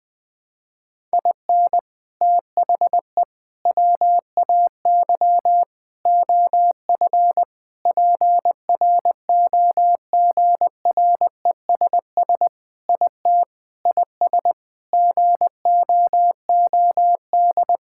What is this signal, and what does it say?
Morse code, heard as a keyed tone, about 21 wpm, 700 Hz. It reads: IN THE WAY OF PROGRESS IT IS GOOD
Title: Sentences with 4-Word Phrases - 20wpm